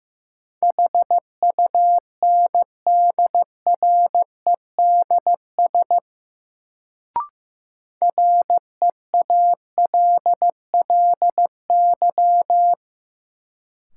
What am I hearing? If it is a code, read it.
Morse code, heard as a keyed tone, about 15 wpm, 700 Hz. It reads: HUNDREDS REALLY
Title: Sets of 4 Words - New Word Drill - 15wpm